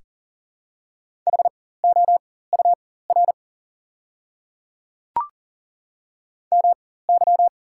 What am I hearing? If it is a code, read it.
Morse code, heard as a keyed tone, about 40 wpm, spaced as for 19 wpm, 690 Hz. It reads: HOUR MY
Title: 4x Character Spacing - Top 500 Words - 40wpm